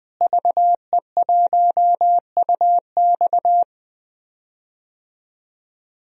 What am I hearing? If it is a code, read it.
Morse code, heard as a keyed tone, about 20 wpm, 700 Hz. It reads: VE1UX
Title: DXCC-100 Call Signs - 5 Characters - Repeated 3x - 20wpm